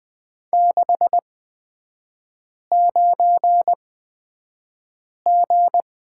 Morse code, heard as a keyed tone, about 20 wpm, spaced as for 5 wpm, 700 Hz. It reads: 69G
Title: 255 Lesson 41e - Period - Three Characters 20wpm